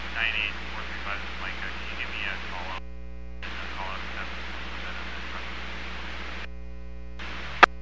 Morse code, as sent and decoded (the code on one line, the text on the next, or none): none